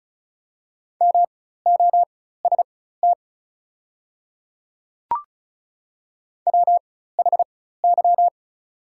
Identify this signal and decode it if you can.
Morse code, heard as a keyed tone, about 36 wpm, spaced as for 16 wpm, 690 Hz. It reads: MOST WHY
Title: 4x Character Spacing - Top 200 Words - 35wpm